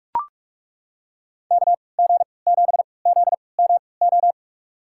Morse code, heard as a keyed tone, about 42 wpm, spaced as for 25 wpm, 700 Hz. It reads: KG7ZMO
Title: USA Call Signs - 6 Characters - 3x Character Spacing - 45wpm